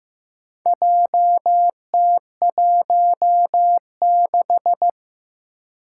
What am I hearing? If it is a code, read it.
Morse code, heard as a keyed tone, about 15 wpm, 700 Hz. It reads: JT16
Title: Four Challenging Letter-Number Combinations 15wpm